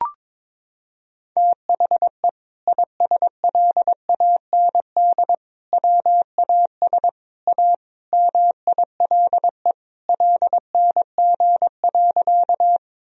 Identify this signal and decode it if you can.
Morse code, heard as a keyed tone, about 22 wpm, 700 Hz. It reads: THE ISLAND WAS A MILE LNG.
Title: Sentences from Top 500 Words and Abbreviations 22wpm